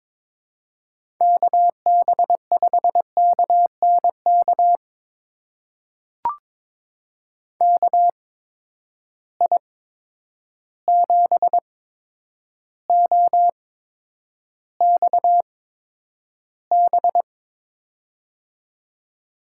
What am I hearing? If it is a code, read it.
Morse code, heard as a keyed tone, about 22 wpm, spaced as for 9 wpm, 700 Hz. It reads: KB5KNK KI7OXB
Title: USA Call Signs - 6 Characters - 8x Character Spacing - 22wpm